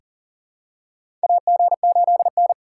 Morse code, heard as a keyed tone, about 40 wpm, 700 Hz. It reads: AG8N
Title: USA Call Signs - 4 Characters - 3x Character Spacing - 40wpm